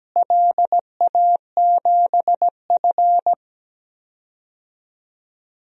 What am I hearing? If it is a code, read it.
Morse code, heard as a keyed tone, about 17 wpm, 700 Hz. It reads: LA7F
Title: DXCC-100 Call Signs - 4 Characters - Repeated 2x - 17wpm